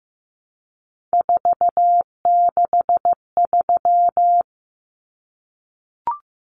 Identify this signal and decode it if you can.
Morse code, heard as a keyed tone, about 15 wpm, 700 Hz. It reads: 463
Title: Common Cut Numbers - 3-Digit 15wpm